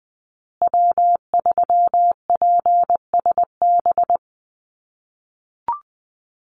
Morse code, heard as a keyed tone, about 20 wpm, 700 Hz. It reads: W3PSB
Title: USA Call Signs Repeated 2x 20wpm